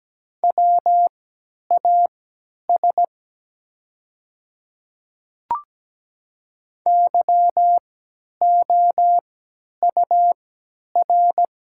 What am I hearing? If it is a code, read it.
Morse code, heard as a keyed tone, about 17 wpm, spaced as for 10 wpm, 690 Hz. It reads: WAS YOUR
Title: 3x Character Spacing - Top 300 Words - 17wpm